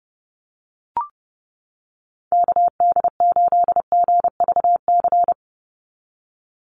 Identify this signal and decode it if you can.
Morse code, heard as a keyed tone, about 30 wpm, 700 Hz. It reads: KD8GVC
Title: USA Call Signs 30wpm